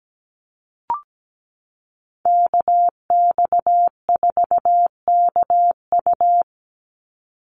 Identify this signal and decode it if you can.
Morse code, heard as a keyed tone, about 17 wpm, 700 Hz. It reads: KX4KU
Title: Call Signs - 17wpm